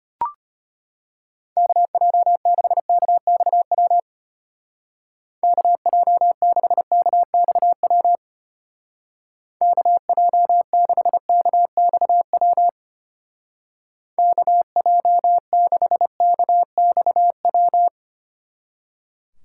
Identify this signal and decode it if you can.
Morse code, heard as a keyed tone, about 32 wpm, 700 Hz. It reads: KJ6KXW KJ6KXW KJ6KXW KJ6KXW